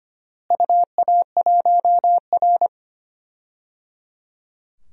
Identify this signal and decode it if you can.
Morse code, heard as a keyed tone, about 25 wpm, 700 Hz. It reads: UA1R
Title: DXCC-100 Call Signs - 4 Characters - Repeated 3x - 25wpm